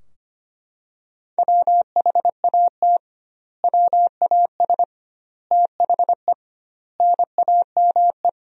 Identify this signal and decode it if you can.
Morse code, heard as a keyed tone, about 25 wpm, 700 Hz. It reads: WHAT WAS THE NAME
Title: 2x Word-Spacing - Sentences from Top 300 Words - 25wpm